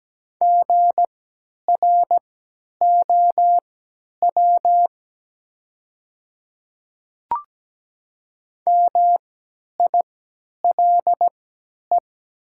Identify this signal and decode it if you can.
Morse code, heard as a keyed tone, about 17 wpm, spaced as for 10 wpm, 690 Hz. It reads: GROW MILE
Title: 3x Character Spacing - Top 300 Words - 17wpm